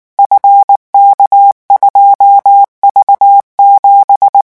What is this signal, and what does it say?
Morse code, heard as a keyed tone, about 19 wpm, 800 Hz. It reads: FK2V7
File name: morse mp3 / revision12-vitesse 18 mots